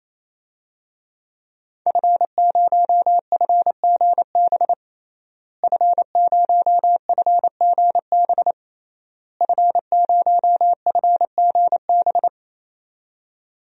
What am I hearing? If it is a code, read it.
Morse code, heard as a keyed tone, about 28 wpm, 690 Hz. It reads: F0FGB F0FGB F0FGB
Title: DXCC-100 Call Signs - 5 Characters - Repeated 3x - 28wpm